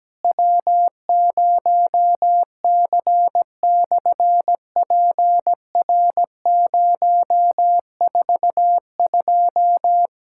Morse code, heard as a keyed tone, about 17 wpm, 700 Hz. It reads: W0C/PR042
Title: SOTA QSO Practice 17wpm